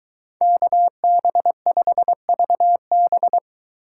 Morse code, heard as a keyed tone, about 22 wpm, 700 Hz. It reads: KB5VB